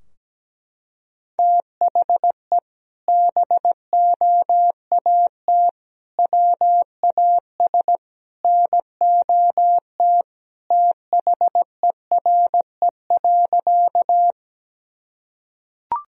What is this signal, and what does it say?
Morse code, heard as a keyed tone, about 17 wpm, 700 Hz. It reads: THE BOAT WAS NOT THERE.
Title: Sentences from Top 1000 Words - Review Part 3 - 17wpm